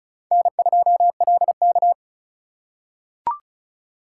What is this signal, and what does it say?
Morse code, heard as a keyed tone, about 34 wpm, 700 Hz. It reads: N2LK
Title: K1USN SST QSO Practice - 35wpm